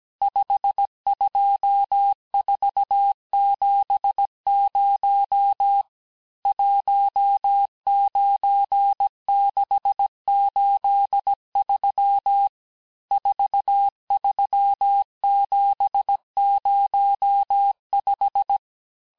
Morse code, heard as a keyed tone, about 17 wpm, 800 Hz. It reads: 52470 19683 43705
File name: corrige-cours-1581-texte-chiffre-vitesse-16-mpm